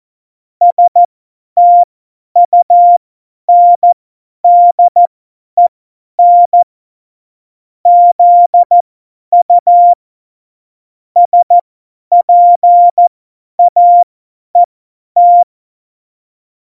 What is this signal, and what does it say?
Morse code, heard as a keyed tone, about 14 wpm, spaced as for 10 wpm, 700 Hz. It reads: STUNDEN ZU SPAET
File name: Morsekurs / Witze / Witz 14. Mai 2019, 50 F70